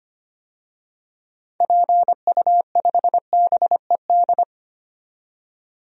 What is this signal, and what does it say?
Morse code, heard as a keyed tone, about 25 wpm, 700 Hz. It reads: PU5BED
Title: DXCC-100 Call Signs - 6 Characters - 25wpm